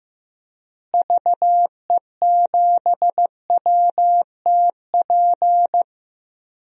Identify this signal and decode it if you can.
Morse code, heard as a keyed tone, about 16 wpm, 700 Hz. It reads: VE7WTP
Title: DXCC-100 Call Signs - 6 Characters - Repeated 3x - 15wpm